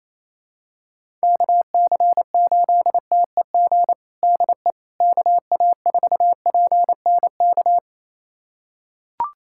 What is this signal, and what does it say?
Morse code, heard as a keyed tone, about 28 wpm, 700 Hz. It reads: KC8TEG DE KA4PNK
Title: USA Callsign DE Callsign 28wpm